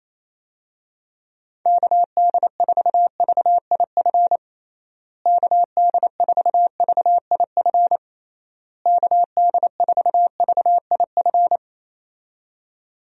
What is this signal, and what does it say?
Morse code, heard as a keyed tone, about 28 wpm, 700 Hz. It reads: KD4VIF KD4VIF KD4VIF
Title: USA Call Signs - 6 Characters - Repeated 3x - 28wpm